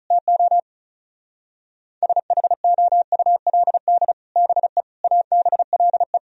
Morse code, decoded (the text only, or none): TO SHOULD BE ABLE